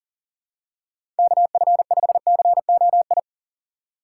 Morse code, heard as a keyed tone, about 40 wpm, 700 Hz. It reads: KF5COI
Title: USA Call Signs - 6 Characters - 3x Character Spacing - 40wpm